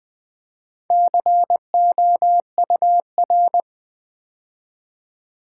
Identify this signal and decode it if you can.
Morse code, heard as a keyed tone, about 20 wpm, 700 Hz. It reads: COUR
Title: Build A Word - 2 Letters At A Time - 8-Letter Words - 20wpm